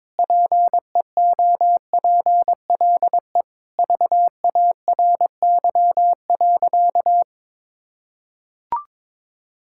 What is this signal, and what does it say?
Morse code, heard as a keyed tone, about 22 wpm, 700 Hz. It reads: PEOPLE VARY.
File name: practice / 22wpm / Sentences with 2-Word Phrases - 22wpm